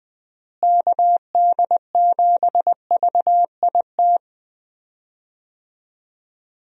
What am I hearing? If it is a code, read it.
Morse code, heard as a keyed tone, about 20 wpm, 700 Hz. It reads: KD7VIT
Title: USA Call Signs - 6 Characters - Repeated 2x - 20wpm